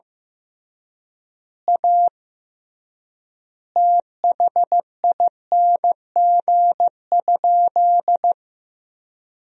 Morse code, heard as a keyed tone, about 16 wpm, 700 Hz. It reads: A THING?
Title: 3x Word-Spacing - Sentences from Top 200 Words - 15wpm